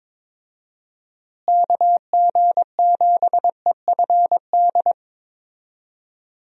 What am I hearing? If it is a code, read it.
Morse code, heard as a keyed tone, about 21 wpm, 700 Hz. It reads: KG7EFD
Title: USA Call Signs - 6 Characters - Repeated 3x - 22wpm